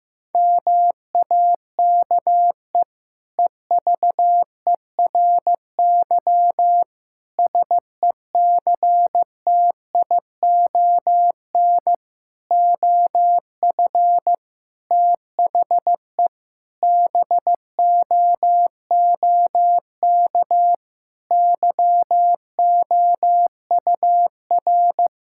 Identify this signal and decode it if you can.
Morse code, heard as a keyed tone, about 15 wpm, 690 Hz. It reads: MAKE EVERY SECTION OF THE BOOK YOUR
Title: Sentences from Top 1000 Words - Review Part 5 - 15wpm